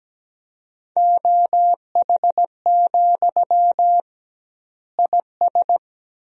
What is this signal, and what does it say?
Morse code, heard as a keyed tone, about 17 wpm, 700 Hz. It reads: OH, IS
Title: 2x Word-Spacing - Sentences from Top 500 Words - 17wpm